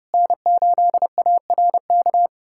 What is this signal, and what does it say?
Morse code, heard as a keyed tone, about 30 wpm, 700 Hz. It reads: N8ARK
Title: USA Callsign DE Callsign 30wpm